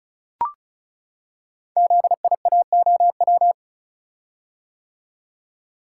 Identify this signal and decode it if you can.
Morse code, heard as a keyed tone, about 36 wpm, 700 Hz. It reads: ZIAOW
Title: Code Groups - 5 Letter:Numbers - 35wpm